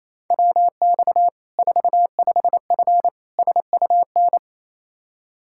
Morse code, heard as a keyed tone, about 28 wpm, 700 Hz. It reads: WX 45F SUN
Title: Practice QSO - Exchange 2 - Info - 28wpm